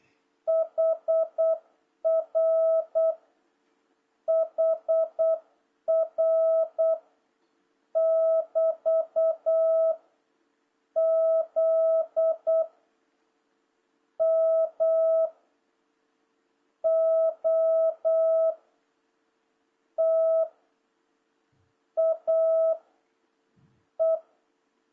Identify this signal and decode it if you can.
Morse code, heard as a keyed tone, about 8 wpm, 660 Hz. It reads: HR HR <BT> Z M O T A E
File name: jh8cbh / 2026cw